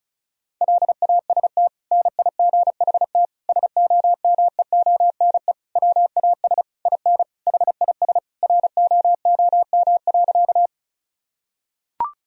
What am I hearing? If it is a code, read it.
Morse code, heard as a keyed tone, about 36 wpm, 700 Hz. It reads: LAST NIGHT SOMEONE WAS IN HIS ROOM.
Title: Sentences from Top 1000 Words - Review Part 4 - 35wpm